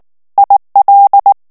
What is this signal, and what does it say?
Morse code, heard as a keyed tone, about 19 wpm, 800 Hz. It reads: IL
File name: CW / FICHIERS-MP3 / FRANCAIS TEXTE CLAIR 2 VITESSE 18 MOTS MINUTES